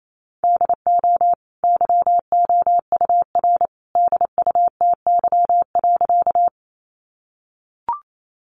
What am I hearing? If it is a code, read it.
Morse code, heard as a keyed tone, about 28 wpm, 700 Hz. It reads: DO YOUR DUTY.
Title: Sentences from Top 1500 Words 28wpm